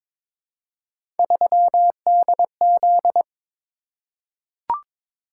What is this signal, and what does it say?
Morse code, heard as a keyed tone, about 22 wpm, 700 Hz. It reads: 3DZ
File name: practice / 22wpm / Three Letter-Number Combinations - 22wpm